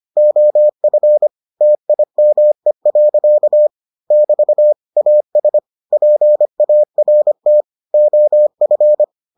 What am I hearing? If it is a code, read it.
Morse code, heard as a keyed tone, about 25 wpm, 600 Hz. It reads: OF TIME. <BT> AS PART OF